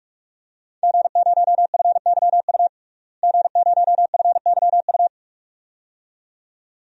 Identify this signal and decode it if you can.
Morse code, heard as a keyed tone, about 43 wpm, 700 Hz. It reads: G0FYU G0FYU
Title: DXCC-100 Call Signs - 5 Characters - Repeated 2x - 45wpm